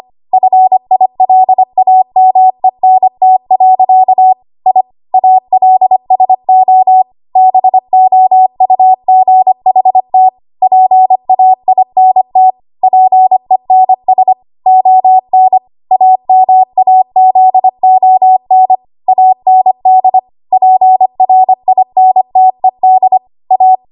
Morse code, heard as a keyed tone, about 25 wpm, 750 Hz. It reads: FILAMENT. I ALSO BOUGHT PAINT PENS ON AMAZON AND PRINTED A
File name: Archive / 25 WPM / 25WPM